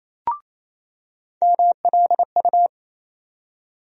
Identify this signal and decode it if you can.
Morse code, heard as a keyed tone, about 27 wpm, 700 Hz. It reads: MLU
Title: USA Call Sign Postfixes - 3 Letters 28wpm